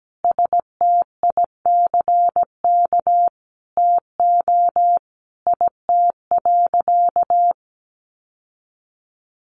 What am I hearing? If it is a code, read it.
Morse code, heard as a keyed tone, about 17 wpm, 700 Hz. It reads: STICK TO IT.
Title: Sentences from Top 1200 Words 17wpm